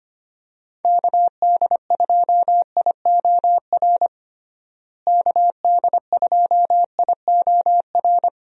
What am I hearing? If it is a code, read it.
Morse code, heard as a keyed tone, about 25 wpm, 700 Hz. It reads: KD2IOR KD2IOR
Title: USA Call Signs - 6 Characters - Repeated 2x - 25wpm